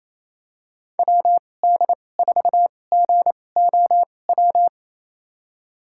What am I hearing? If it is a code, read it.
Morse code, heard as a keyed tone, about 28 wpm, spaced as for 20 wpm, 700 Hz. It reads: WD4GOW
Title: USA Call Signs - 6 Characters - 2x Character Spacing - 28wpm